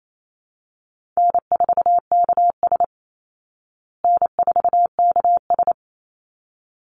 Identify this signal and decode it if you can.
Morse code, heard as a keyed tone, about 28 wpm, 700 Hz. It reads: N4KS N4KS
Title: Easy Call Signs Repeated 2x - 28wpm